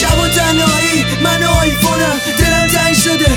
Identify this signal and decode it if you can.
There is no code.